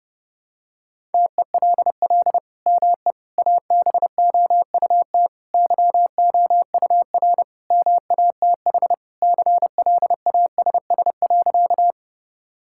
Morse code, heard as a keyed tone, about 32 wpm, 700 Hz. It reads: TELL ME ABOUT YOUR MATH CLASS.
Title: Sentences from Top 1000 Words - Review Part 6 - 30wpm